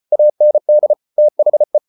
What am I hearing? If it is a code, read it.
Morse code, heard as a keyed tone, about 34 wpm, 600 Hz. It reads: AND THE